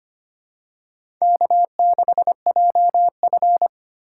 Morse code, heard as a keyed tone, about 25 wpm, 700 Hz. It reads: K6JF
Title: USA Call Signs - 4 Characters - 3x Character Spacing - 25wpm